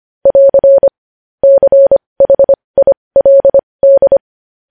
Morse code, heard as a keyed tone, about 25 wpm, spaced as for 20 wpm, 550 Hz. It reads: <AR> CHILD
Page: news headlines in morse code at 20 wpm